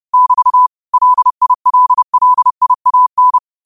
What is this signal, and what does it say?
Morse code, heard as a keyed tone, about 30 wpm, 1000 Hz. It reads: <BT> LILLIAN